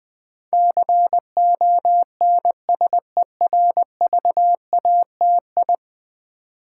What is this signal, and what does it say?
Morse code, heard as a keyed tone, about 20 wpm, 700 Hz. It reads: CONSERVATI
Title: Build A Word - 5 Letters At A Time - 12-Letter Words - 20wpm